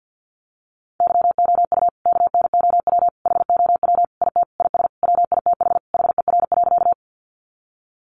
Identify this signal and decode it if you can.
Morse code, heard as a keyed tone, about 50 wpm, 700 Hz. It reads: YOU KNOW HOW IT IS WITH HER.
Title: Sentences from Top 100 Words 50wpm